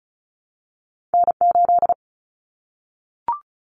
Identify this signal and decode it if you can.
Morse code, heard as a keyed tone, about 36 wpm, 700 Hz. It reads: N8
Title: Common Cut Numbers - 2-Digit 35wpm